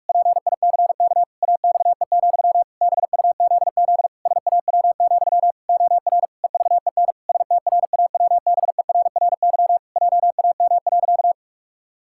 Morse code, heard as a keyed tone, about 46 wpm, 700 Hz. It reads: PICK AXE, BUZZ SAW, OR EVEN STRAWBERRY JAM.
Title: Pangram Challenge v2 - 45wpm